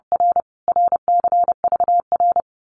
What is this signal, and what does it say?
Morse code, heard as a keyed tone, about 30 wpm, 700 Hz. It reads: R RCVR
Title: Abbreviations Repeated 3x - 30wpm